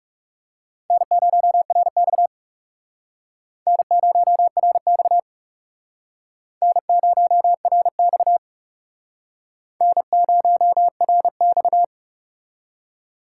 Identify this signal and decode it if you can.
Morse code, heard as a keyed tone, about 39 wpm, 700 Hz. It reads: N0RX N0RX N0RX N0RX